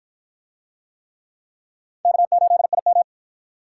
Morse code, heard as a keyed tone, about 53 wpm, 700 Hz. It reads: K8IM